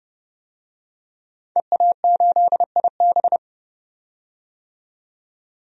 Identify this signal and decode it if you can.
Morse code, heard as a keyed tone, about 32 wpm, 700 Hz. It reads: EA8IB
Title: DXCC-100 Call Signs - 5 Characters - 30wpm